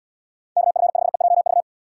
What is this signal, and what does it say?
Morse code, heard as a keyed tone, about 78 wpm, 700 Hz. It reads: KF5EJV